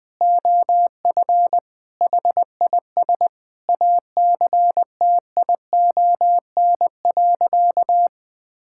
Morse code, heard as a keyed tone, about 20 wpm, 700 Hz. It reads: OF HIS ACTION.
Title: Sentences from Top 1000 Words - Review Part 5 - 20wpm